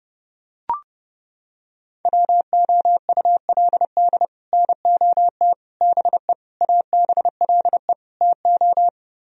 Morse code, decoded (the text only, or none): WOULD NOT BE ABLE TO